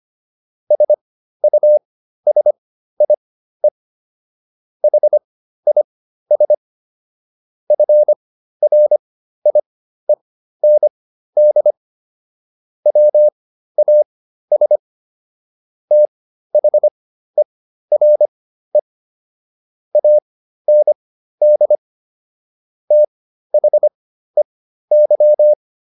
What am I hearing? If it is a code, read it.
Morse code, heard as a keyed tone, about 25 wpm, spaced as for 13 wpm, 600 Hz. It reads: SUSIE HIS FRIEND WAS THERE AND THEY